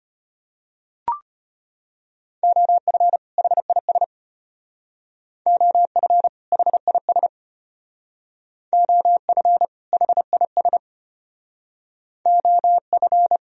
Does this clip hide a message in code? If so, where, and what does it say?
Yes: Morse code from 2.4 to 13.4 s, heard as a keyed tone, about 33 wpm, 700 Hz. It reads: OF HIS OF HIS OF HIS OF